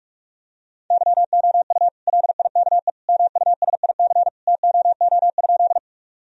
Morse code, heard as a keyed tone, about 46 wpm, 700 Hz. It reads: YOU LIKE MUSIC TOO?
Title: Sentences from Top 1000 Words - Review Part 1 - 45wpm